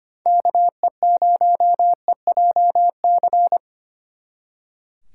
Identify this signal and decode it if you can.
Morse code, heard as a keyed tone, about 30 wpm, 700 Hz. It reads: KE0EJC
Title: Speed Racing - USA Call Signs 25wpm